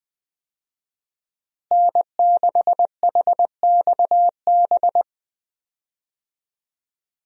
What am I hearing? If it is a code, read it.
Morse code, heard as a keyed tone, about 20 wpm, 700 Hz. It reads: N6HXB
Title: USA Call Signs - 5 Characters - Repeated 3x - 20wpm